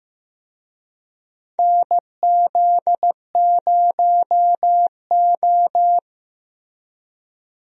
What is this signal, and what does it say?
Morse code, heard as a keyed tone, about 15 wpm, 700 Hz. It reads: NZ0O
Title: USA Call Signs - 4 Characters - Repeated 2x - 15wpm